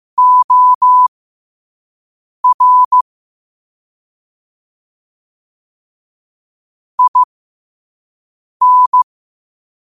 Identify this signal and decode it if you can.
Morse code, heard as a keyed tone, about 15 wpm, spaced as for 5 wpm, 1000 Hz. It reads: OR IN
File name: qotdfiles / 5wpm